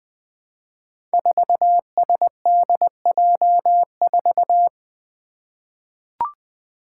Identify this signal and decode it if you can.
Morse code, heard as a keyed tone, about 20 wpm, 700 Hz. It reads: 4SDJ4
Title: Five Challenging Letter-Number Combinations 20wpm